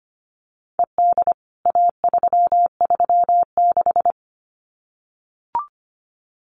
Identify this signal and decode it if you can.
Morse code, heard as a keyed tone, about 25 wpm, 700 Hz. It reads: ED A336
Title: CWT Exchange 25wpm